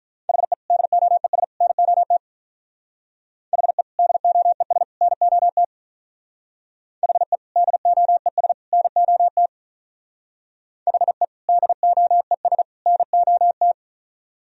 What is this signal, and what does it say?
Morse code, heard as a keyed tone, about 44 wpm, 700 Hz. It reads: HE DOES NOT HE DOES NOT HE DOES NOT HE DOES NOT
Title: Speed-Racing - Sets of 3 Words 35wpm